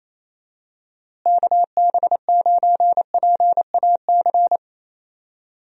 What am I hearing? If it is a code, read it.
Morse code, heard as a keyed tone, about 28 wpm, 700 Hz. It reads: KB9PAC
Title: USA Call Signs - 6 Characters - Repeated 2x - 28wpm